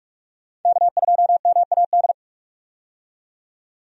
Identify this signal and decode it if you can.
Morse code, heard as a keyed tone, about 45 wpm, 700 Hz. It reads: K2MAD
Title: USA Call Signs - 5 Characters - 45wpm